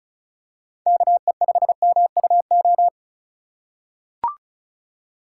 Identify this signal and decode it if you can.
Morse code, heard as a keyed tone, about 36 wpm, 700 Hz. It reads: KE5MUO
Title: USA Call Signs - 6 Characters - 2x Character Spacing - 35wpm